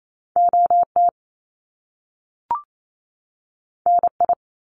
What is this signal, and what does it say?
Morse code, heard as a keyed tone, about 28 wpm, 700 Hz. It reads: OT NI
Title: Top Two Letter Combinations in Words 28wpm